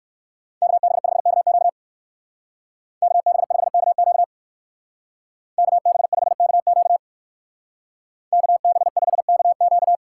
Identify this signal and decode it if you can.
Morse code, heard as a keyed tone, about 55 wpm, 700 Hz. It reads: KB5KQ KB5KQ KB5KQ KB5KQ
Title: Speed Racing - USA 2x2 Call Signs 45wpm